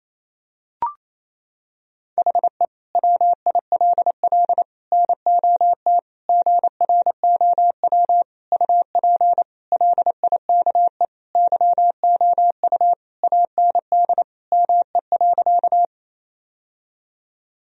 Morse code, heard as a keyed tone, about 28 wpm, 700 Hz. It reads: HE WILL NOT GROW UP LIKE YOU AND ME.
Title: Sentences from Top 1000 Words - Review Part 1 - 28wpm